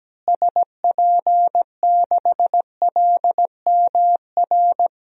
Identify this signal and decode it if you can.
Morse code, heard as a keyed tone, about 17 wpm, 700 Hz. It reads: SP6LMR
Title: DXCC-100 Call Signs - 6 Characters - Repeated 2x - 17wpm